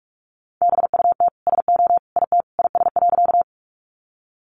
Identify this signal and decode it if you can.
Morse code, heard as a keyed tone, about 44 wpm, 700 Hz. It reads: BUT SO IT IS.
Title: Sentences from Top 100 Words 45wpm